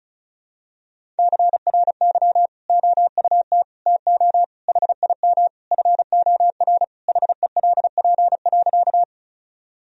Morse code, heard as a keyed tone, about 36 wpm, 700 Hz. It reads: CRY OUT TO HIM FOR HELP.
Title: Sentences from Top 1000 Words - Review Part 2 - 35wpm